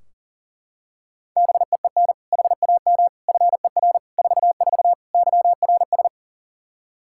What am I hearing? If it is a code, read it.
Morse code, heard as a keyed tone, about 40 wpm, 700 Hz. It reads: BEEN HAM FER 44 YRS
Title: Practice QSO - Exchange 3 - More Info - 40wpm